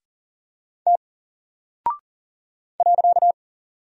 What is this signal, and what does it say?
Morse code, heard as a keyed tone, about 40 wpm, 700 Hz. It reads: T .
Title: Single Character (Letter:Number:Punctuation:Prosign) Alt 40wpm